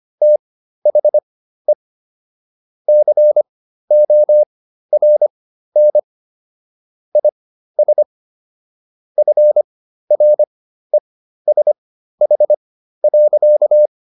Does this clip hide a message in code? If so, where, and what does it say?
Yes: Morse code from 0.2 to 13.9 s, heard as a keyed tone, about 25 wpm, spaced as for 13 wpm, 600 Hz. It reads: THE CORN IS FRESH.